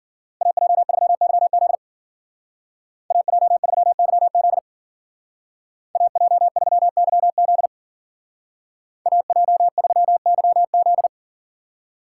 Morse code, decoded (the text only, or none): AJ3YZ AJ3YZ AJ3YZ AJ3YZ